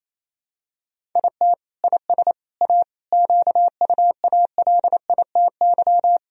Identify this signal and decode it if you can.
Morse code, heard as a keyed tone, about 28 wpm, 700 Hz. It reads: IT IS A QUALITY
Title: Sentences from Top 1000 Words - Review Part 4 - 28wpm